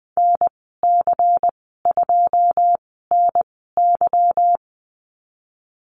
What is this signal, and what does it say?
Morse code, heard as a keyed tone, about 20 wpm, spaced as for 15 wpm, 700 Hz. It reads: NC2NY
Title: 173 Lesson 28g - Number 9 - Call Signs 20wpm